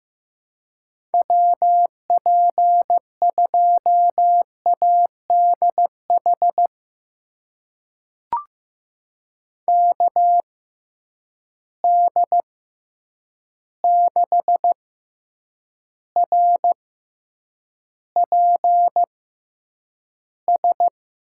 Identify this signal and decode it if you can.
Morse code, heard as a keyed tone, about 15 wpm, spaced as for 8 wpm, 700 Hz. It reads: WP2ADH KD6RPS
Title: USA Call Signs - 6 Characters - 6x Character Spacing - 15wpm